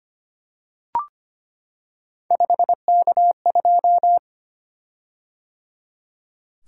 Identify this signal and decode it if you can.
Morse code, heard as a keyed tone, about 25 wpm, 700 Hz. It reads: 5K2
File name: Three Letter-Number Combinations - 25wpm